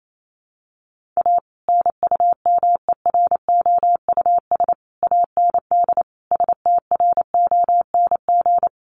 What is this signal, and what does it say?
Morse code, heard as a keyed tone, about 28 wpm, 700 Hz. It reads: A NUMEROUS AND STRONG
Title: Sentences from Top 1400 Words 28wpm